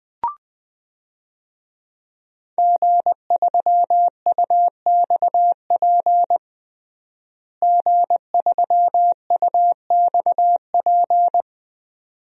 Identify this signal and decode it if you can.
Morse code, heard as a keyed tone, about 20 wpm, 700 Hz. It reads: G3UXP G3UXP
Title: DXCC-100 Call Signs - 5 Characters - Repeated 2x - 20wpm